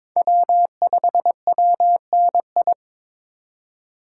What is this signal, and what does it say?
Morse code, heard as a keyed tone, about 21 wpm, 700 Hz. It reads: W5WNI
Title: USA Call Signs - 5 Characters - 2x Character Spacing - 22wpm